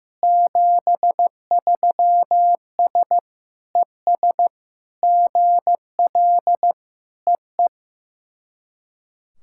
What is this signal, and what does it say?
Morse code, heard as a keyed tone, about 15 wpm, 700 Hz. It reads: 73S ES GL EE
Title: SOTA QSO Practice 15wpm